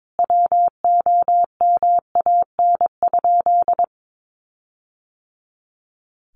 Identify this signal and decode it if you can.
Morse code, heard as a keyed tone, about 22 wpm, 700 Hz. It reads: WOMAN?
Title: Sentences from Top 1400 Words 22wpm